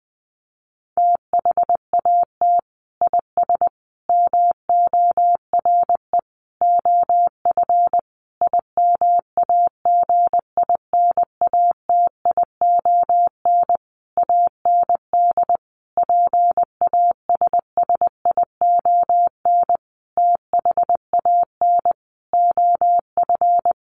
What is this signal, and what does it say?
Morse code, heard as a keyed tone, about 20 wpm, 700 Hz. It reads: THAT IS MORE OF IMAGINATION AND PASSION THAN OF
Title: Sentences from Top 1300 Words 20wpm